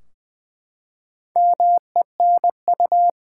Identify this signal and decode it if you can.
Morse code, heard as a keyed tone, about 20 wpm, 700 Hz. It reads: MENU
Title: Four Letter Words - 20wpm